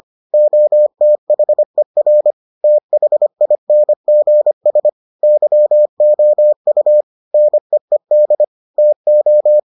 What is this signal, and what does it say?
Morse code, heard as a keyed tone, about 25 wpm, 600 Hz. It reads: OTHER THINGS YOU NEED TO